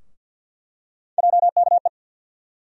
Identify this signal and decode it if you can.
Morse code, heard as a keyed tone, about 51 wpm, 700 Hz. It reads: JKE
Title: USA Call Sign Postfixes - 3 Letters 50wpm